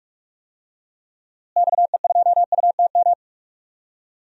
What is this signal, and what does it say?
Morse code, heard as a keyed tone, about 41 wpm, 700 Hz. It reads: XE2UTM
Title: DXCC-100 Call Signs - 6 Characters - Repeated 2x - 45wpm